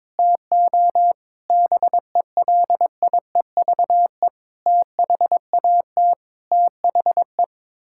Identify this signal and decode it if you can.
Morse code, heard as a keyed tone, about 21 wpm, 700 Hz. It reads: TO BELIEVE THAT THE
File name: practice / 22wpm / Sets of 8 Words - 22wpm